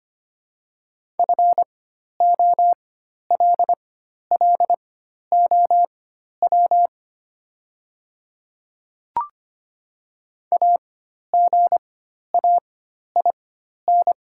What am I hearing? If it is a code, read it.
Morse code, heard as a keyed tone, about 25 wpm, spaced as for 12 wpm, 700 Hz. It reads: FOLLOW AGAIN
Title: 4x Character Spacing - Top 200 Words - 25wpm